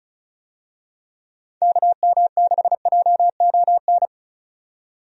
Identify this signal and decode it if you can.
Morse code, heard as a keyed tone, about 36 wpm, 700 Hz. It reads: KM6JON
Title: USA Call Signs - 6 Characters - 35wpm